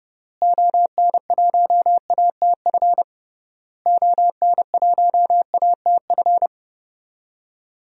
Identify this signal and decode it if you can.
Morse code, heard as a keyed tone, about 30 wpm, 700 Hz. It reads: ON1ATF ON1ATF
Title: DXCC-100 Call Signs - 6 Characters - Repeated 3x - 30wpm